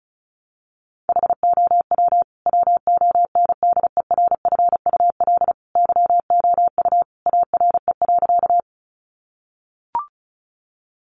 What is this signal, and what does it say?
Morse code, heard as a keyed tone, about 34 wpm, 700 Hz. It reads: HOW WONDERFUL YOU ARE.
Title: Sentences from Top 1100 Words 35wpm